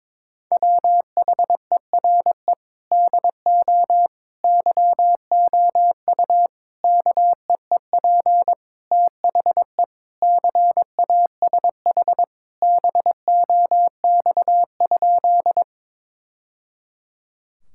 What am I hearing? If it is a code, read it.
Morse code, heard as a keyed tone, about 21 wpm, 700 Hz. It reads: WHERE DO YOU KEEP THE CASH BOX?
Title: Sentences from Top 1000 Words - Review Part 5 - 22wpm